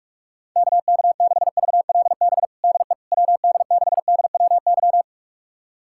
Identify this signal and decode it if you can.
Morse code, heard as a keyed tone, about 46 wpm, 700 Hz. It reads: KK6VLB DE WD6DWY
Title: USA Callsign DE Callsign - 6 Characters - 45wpm